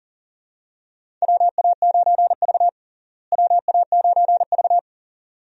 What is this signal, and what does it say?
Morse code, heard as a keyed tone, about 40 wpm, 700 Hz. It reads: WA9V WA9V
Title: USA Call Signs - 4 Characters - Repeated 2x - 40wpm